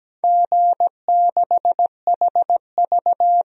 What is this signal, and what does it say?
Morse code, heard as a keyed tone, about 17 wpm, 700 Hz. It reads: G6HV
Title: DXCC-100 Call Signs - 4 Characters - 17wpm